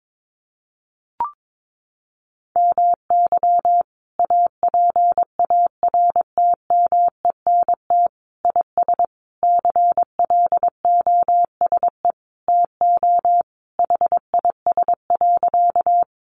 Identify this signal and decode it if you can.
Morse code, heard as a keyed tone, about 22 wpm, 700 Hz. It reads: MY APARTMENT IS CLOSE TO HIS.
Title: Sentences from Top 900 Words 22wpm